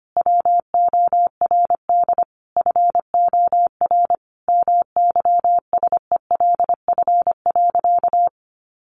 Morse code, heard as a keyed tone, about 25 wpm, 700 Hz. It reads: WORD FOR MYSELF.
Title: Sentences from Top 1300 Words 25wpm